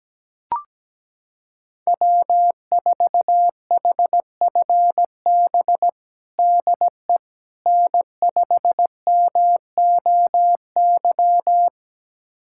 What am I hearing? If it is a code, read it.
Morse code, heard as a keyed tone, about 17 wpm, 700 Hz. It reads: W4HFB DE N5MOY
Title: USA Callsign DE Callsign - 5 Characters - 17wpm